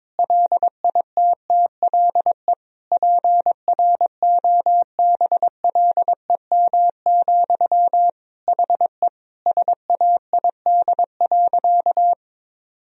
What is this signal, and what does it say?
Morse code, heard as a keyed tone, about 22 wpm, 700 Hz. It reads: LITTLE PROBLEM, HE SAID.
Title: Sentences from Top 1000 Words - Review Part 2 - 22wpm